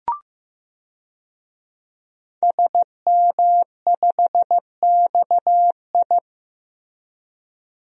Morse code, heard as a keyed tone, about 16 wpm, 700 Hz. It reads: SM5XI
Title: DXCC-100 Call Signs - 5 Characters - Repeated 2x - 15wpm